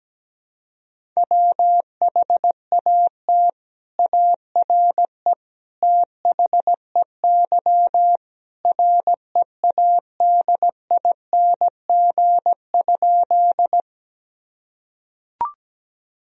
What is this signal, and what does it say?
Morse code, heard as a keyed tone, about 17 wpm, 700 Hz. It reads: WHAT ARE THEY READING?
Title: Sentences from Top 1000 Words - Review Part 3 - 17wpm